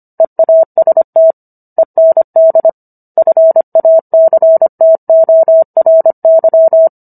Morse code, heard as a keyed tone, about 25 wpm, 650 Hz. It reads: EAST END FACTORY
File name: Book One / 16-Chapter 16 - 25 WPM